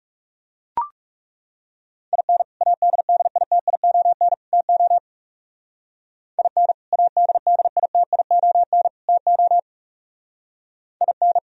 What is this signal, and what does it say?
Morse code, heard as a keyed tone, about 37 wpm, 700 Hz. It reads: IN ADDITION TO IN ADDITION TO IN
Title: Speed-Racing - Sets of 3 Words 30wpm